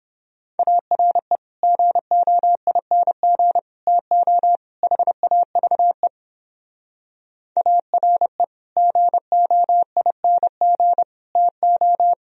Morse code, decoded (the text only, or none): ARE GOING TO HAVE ARE GOING TO